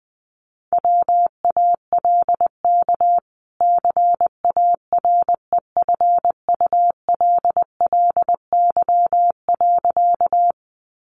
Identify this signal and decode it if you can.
Morse code, heard as a keyed tone, about 19 wpm, 700 Hz. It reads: WALK CAREFULLY.
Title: Sentences from Top 1300 Words 20wpm